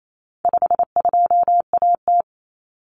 Morse code, heard as a keyed tone, about 28 wpm, 700 Hz. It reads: 52AT
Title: Common Cut Numbers - 4-Digit 28wpm